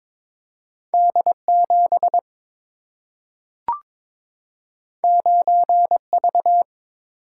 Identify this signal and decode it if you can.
Morse code, heard as a keyed tone, about 21 wpm, 700 Hz. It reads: D7 9V
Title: Single-Letter Single-Number 22wpm